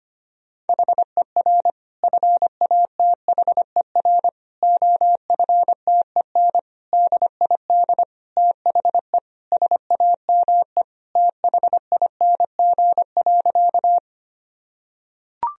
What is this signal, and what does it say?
Morse code, heard as a keyed tone, about 25 wpm, 700 Hz. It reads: HER FATHER OFTEN DID THE SAME THING.
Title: Sentences from Top 1000 Words - Review Part 1 - 25wpm